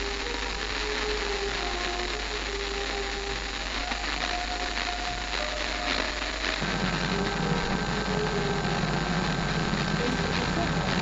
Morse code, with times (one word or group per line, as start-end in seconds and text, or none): none